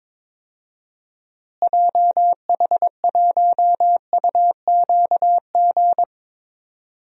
Morse code, heard as a keyed tone, about 22 wpm, 700 Hz. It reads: JH1UQG
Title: DXCC-100 Call Signs - 6 Characters - Repeated 3x - 22wpm